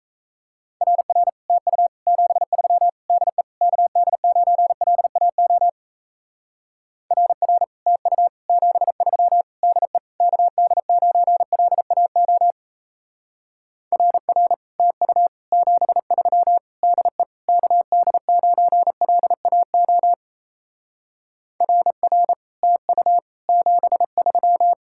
Morse code, heard as a keyed tone, about 34 wpm, 700 Hz. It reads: RR TU 73 DE KD9LAO RR TU 73 DE KD9LAO RR TU 73 DE KD9LAO RR TU 73
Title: Speed-Racing Common QSO Phrases Encore - 28wpm